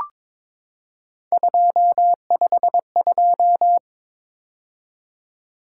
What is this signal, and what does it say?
Morse code, heard as a keyed tone, about 22 wpm, 700 Hz. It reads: 252
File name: practice / 22wpm / Three Letter-Number Combinations - 22wpm